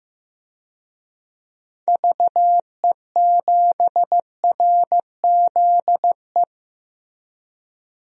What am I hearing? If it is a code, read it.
Morse code, heard as a keyed tone, about 15 wpm, 700 Hz. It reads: VE7RZE
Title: DXCC-100 Call Signs - 6 Characters - Repeated 3x - 15wpm